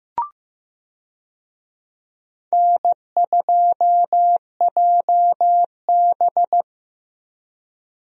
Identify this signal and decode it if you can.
Morse code, heard as a keyed tone, about 16 wpm, 700 Hz. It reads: N2JB
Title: USA Call Signs - 4 Characters - 15wpm